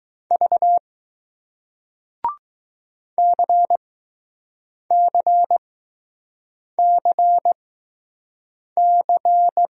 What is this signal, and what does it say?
Morse code, heard as a keyed tone, about 20 wpm, spaced as for 6 wpm, 700 Hz. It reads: V CCCC